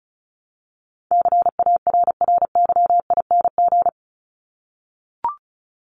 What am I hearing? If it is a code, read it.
Morse code, heard as a keyed tone, about 34 wpm, 700 Hz. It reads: CARRYING
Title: ING Suffix - Top 100 Words with Base Words 35wpm